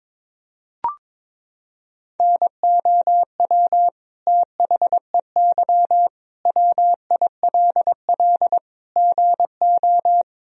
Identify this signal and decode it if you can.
Morse code, heard as a keyed tone, about 21 wpm, 700 Hz. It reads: NOW THEY WILL GO
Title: Sentences from Top 1000 Words - Review Part 4 - 22wpm